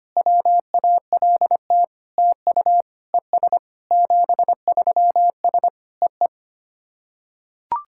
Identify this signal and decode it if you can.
Morse code, heard as a keyed tone, about 25 wpm, 700 Hz. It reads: WALT TU ES 73S EE
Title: SOTA QSO Practice 25wpm